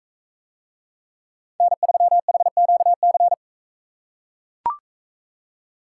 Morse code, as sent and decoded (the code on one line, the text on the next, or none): -. ...-- .... --.- -.-.
N3HQC